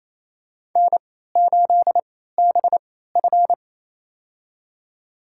Morse code, heard as a keyed tone, about 28 wpm, spaced as for 16 wpm, 700 Hz. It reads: N8BF
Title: USA Call Signs - 4 Characters - 3x Character Spacing - 28wpm